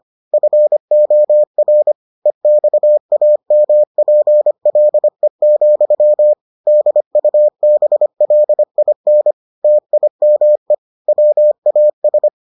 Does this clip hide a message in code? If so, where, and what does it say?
Yes: Morse code from 0.3 to 12.3 s, heard as a keyed tone, about 25 wpm, 600 Hz. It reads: FOR EXAMPLE, DUBLIN TIME WAS